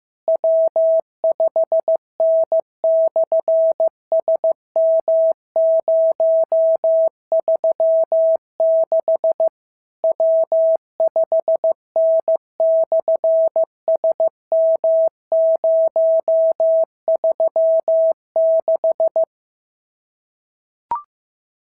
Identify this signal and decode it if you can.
Morse code, heard as a keyed tone, about 15 wpm, 660 Hz. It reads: W5N/SM036 W5N/SM036
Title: SOTA QSO Practice 15wpm